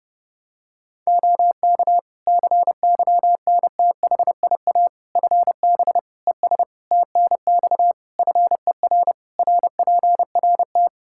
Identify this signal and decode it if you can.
Morse code, heard as a keyed tone, about 32 wpm, 700 Hz. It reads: OK CYNTHIA FB ES TNX FER RPRT
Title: K1USN SST QSO Practice 30wpm